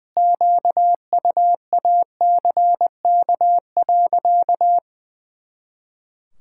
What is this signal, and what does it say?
Morse code, heard as a keyed tone, about 20 wpm, 700 Hz. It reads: QUACK.
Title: Pangram Challenge v2 - 20wpm